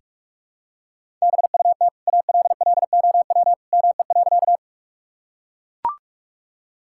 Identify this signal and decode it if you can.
Morse code, heard as a keyed tone, about 43 wpm, 700 Hz. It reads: BUT ALLOW ME.
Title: Sentences from Top 1000 Words - Review Part 5 - 45wpm